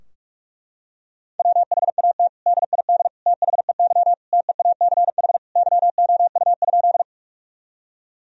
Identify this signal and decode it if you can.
Morse code, heard as a keyed tone, about 41 wpm, 700 Hz. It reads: WHAT DID THEY TEACH YOU?
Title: Sentences from Top 1000 Words - Review Part 2 - 45wpm